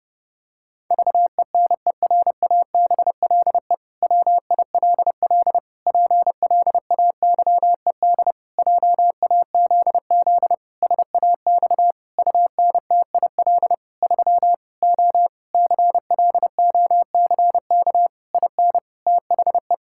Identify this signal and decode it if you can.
Morse code, heard as a keyed tone, about 30 wpm, 700 Hz. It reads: VENERABLE WILL PLAYED JAZZ SAX UNTIL 3 O CLOCK IN THE
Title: Pangram Challenge v2 - 30wpm